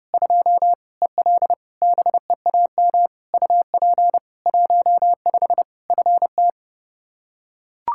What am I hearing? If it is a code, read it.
Morse code, heard as a keyed tone, about 32 wpm, 700 Hz. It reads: 2 EL BEAM UP 15 FT
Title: Practice QSO - Exchange 2 - Info - 30wpm